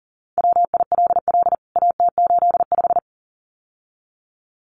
Morse code, heard as a keyed tone, about 40 wpm, 700 Hz. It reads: WILL AT85
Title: CWT Exchange 40wpm